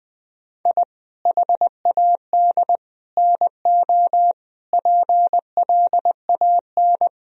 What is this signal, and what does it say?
Morse code, heard as a keyed tone, about 20 wpm, 700 Hz. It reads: I HAD NO PLAN
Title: Sentences from Top 1000 Words - Review Part 2 - 20wpm